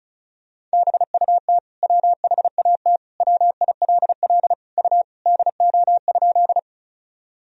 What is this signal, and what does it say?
Morse code, heard as a keyed tone, about 34 wpm, 700 Hz. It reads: BUT WHAT WILL U DO?
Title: Sentences from Top 100 Words and Abbreviations 35wpm